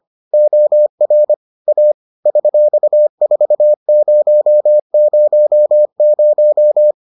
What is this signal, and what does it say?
Morse code, heard as a keyed tone, about 25 wpm, 600 Hz. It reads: OR A $4000